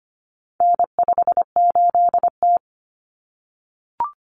Morse code, heard as a keyed tone, about 25 wpm, 700 Hz. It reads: N58T
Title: Common Cut Numbers - 4-Digit 25wpm